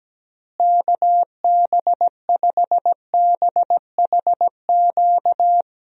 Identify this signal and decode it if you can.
Morse code, heard as a keyed tone, about 17 wpm, 700 Hz. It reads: KB5BHQ
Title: USA Call Signs - 6 Characters - Repeated 3x - 17wpm